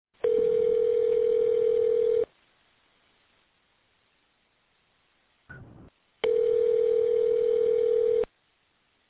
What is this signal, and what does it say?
Ringback tone (North America), 440 + 480 Hz.